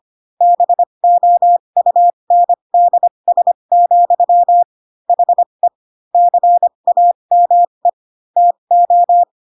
Morse code, heard as a keyed tone, about 25 wpm, spaced as for 22 wpm, 700 Hz. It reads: BOUNDS, HE CAME TO
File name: Pinocchio / Pinocchio ch6 18 25 ik1hge